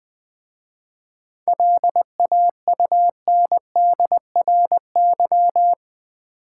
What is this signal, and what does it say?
Morse code, heard as a keyed tone, about 20 wpm, 700 Hz. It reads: LAUNDRY
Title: Easy Words with Hints - 20wpm